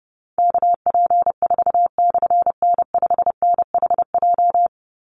Morse code, heard as a keyed tone, about 30 wpm, 700 Hz. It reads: KP4/N5NHJ
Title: Call Signs Repeated 3x - 30wpm